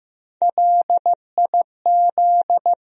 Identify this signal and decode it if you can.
Morse code, heard as a keyed tone, about 15 wpm, 700 Hz. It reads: LIZ
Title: USA Common Names - Max 3 Letters - 15wpm